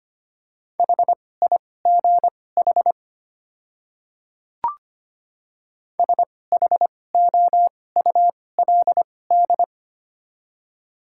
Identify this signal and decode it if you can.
Morse code, heard as a keyed tone, about 25 wpm, spaced as for 18 wpm, 700 Hz. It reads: HIGH SHOULD
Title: 2x Character Spacing - Top 300 Words - 25wpm